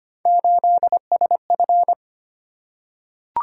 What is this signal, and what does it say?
Morse code, heard as a keyed tone, about 25 wpm, 700 Hz. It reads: 8SF
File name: Three Letter-Number Combinations - 25wpm